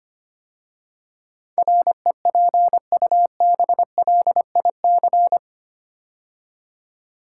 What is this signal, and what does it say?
Morse code, heard as a keyed tone, about 25 wpm, 700 Hz. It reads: REPUBLIC
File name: 8 Letter Words - First 3 Letters Given - 25wpm